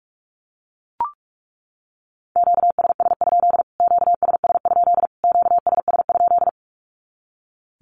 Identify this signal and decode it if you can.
Morse code, heard as a keyed tone, about 44 wpm, 700 Hz. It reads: QSS? QSS? QSS?
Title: Q-Codes Repeated 3x 45wpm